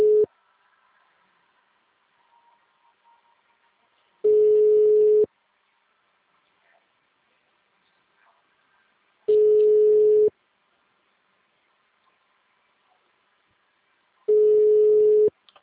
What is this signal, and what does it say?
Ringback tone (Europe, 425 Hz).
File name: out- -2003- - - .250